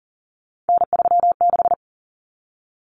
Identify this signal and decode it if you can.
Morse code, heard as a keyed tone, about 40 wpm, 700 Hz. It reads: N36
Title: Common Cut Numbers - 3-Digit 40wpm